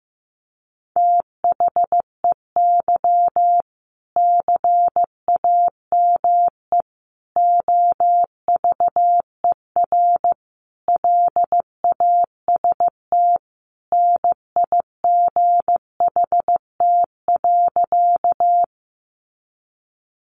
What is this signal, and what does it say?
Morse code, heard as a keyed tone, about 14 wpm, 700 Hz. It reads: THEY CAME OVER LAST NIGHT.
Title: Sentences from Top 300 Words 15wpm